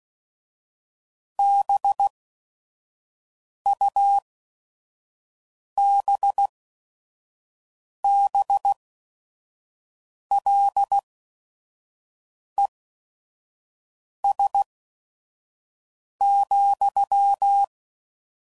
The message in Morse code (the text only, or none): BUBBLES,